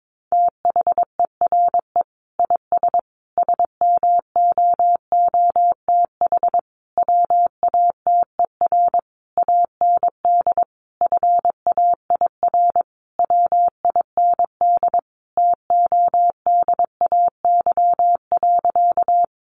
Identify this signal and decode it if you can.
Morse code, heard as a keyed tone, about 22 wpm, 700 Hz. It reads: THERE IS SMOOTH WATER AND FAIR WIND TODAY.
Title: Sentences from Top 1400 Words 22wpm